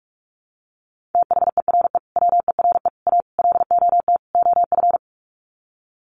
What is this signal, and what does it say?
Morse code, heard as a keyed tone, about 44 wpm, 700 Hz. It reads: THERE WERE A LOT OF
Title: Sets of 5 Words 45wpm